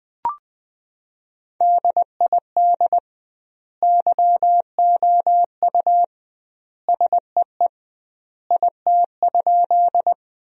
Morse code, heard as a keyed tone, about 20 wpm, 700 Hz. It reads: DID YOU SEE IT?
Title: 2x Word-Spacing - Sentences from Top 100 Words - 20wpm